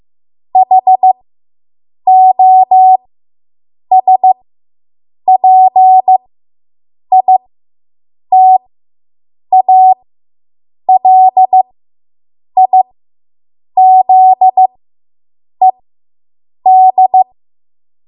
Morse code, heard as a keyed tone, about 15 wpm, spaced as for 7 wpm, 750 Hz. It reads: HOSPITALIZED